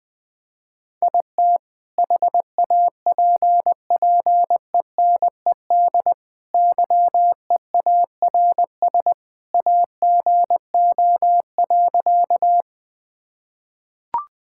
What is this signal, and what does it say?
Morse code, heard as a keyed tone, about 20 wpm, 700 Hz. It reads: IT HAPPENED YEARS AGO.
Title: Sentences with 2-Word Phrases - Spoken Beginning - 20wpm